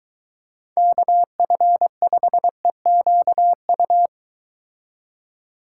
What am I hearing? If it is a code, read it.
Morse code, heard as a keyed tone, about 22 wpm, 700 Hz. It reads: KF5EQU